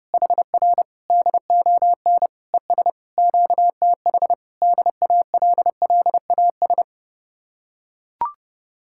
Morse code, heard as a keyed tone, about 30 wpm, 700 Hz. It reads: HR DON ES QTH DALLAS